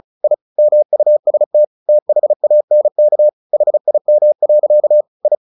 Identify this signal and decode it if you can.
Morse code, heard as a keyed tone, about 34 wpm, 600 Hz. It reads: I MUST THANK HIM. I
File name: the old man and the sea / the old man and the sea 1 35wpm